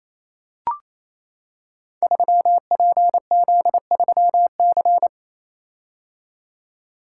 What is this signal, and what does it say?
Morse code, heard as a keyed tone, about 27 wpm, 700 Hz. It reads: 3PZ3C
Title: Code Groups - 5 Letter:Numbers - 28wpm